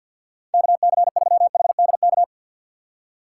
Morse code, heard as a keyed tone, about 52 wpm, 700 Hz. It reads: KC3HDK
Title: USA Call Signs - 6 Characters - 4x Character Spacing - 50wpm